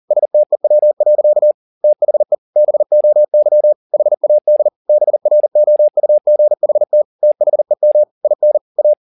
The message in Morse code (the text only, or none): STEW. THE BOY HAD BROUGHT THEM IN A